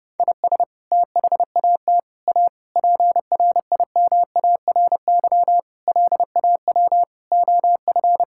Morse code, heard as a keyed tone, about 30 wpm, 700 Hz. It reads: IS THAT A PRIMARY LAW OF
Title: Sentences from Top 1000 Words - Review Part 5 - 30wpm